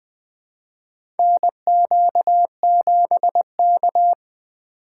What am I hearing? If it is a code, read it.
Morse code, heard as a keyed tone, about 21 wpm, 700 Hz. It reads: NQ7K
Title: USA Call Signs - 6x Character Spacing - 20wpm